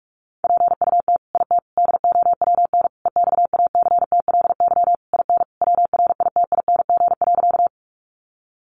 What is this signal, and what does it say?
Morse code, heard as a keyed tone, about 44 wpm, 700 Hz. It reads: PUT IT DOWN EXACTLY IN WRITING.
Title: Sentences from Top 900 Words 45wpm